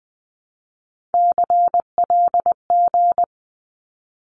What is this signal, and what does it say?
Morse code, heard as a keyed tone, about 20 wpm, 700 Hz. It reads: CLG
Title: Abbreviations 20wpm